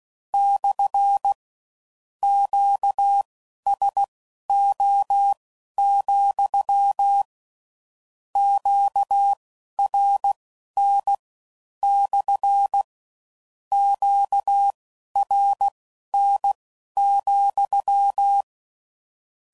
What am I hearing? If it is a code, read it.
Morse code, heard as a keyed tone, about 16 wpm, spaced as for 11 wpm, 800 Hz. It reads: / QSO, QRN / QRN,
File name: CW Code Groups @15WPM / Q Calls / Q Calls SPSL